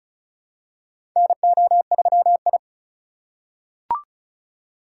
Morse code, heard as a keyed tone, about 34 wpm, 700 Hz. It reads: NO3I
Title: USA Call Signs - 2x Character Spacing - 35wpm